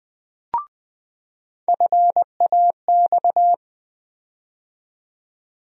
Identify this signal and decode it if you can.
Morse code, heard as a keyed tone, about 20 wpm, 700 Hz. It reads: FAX
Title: USA Call Sign Postfixes - 3 Letters 20wpm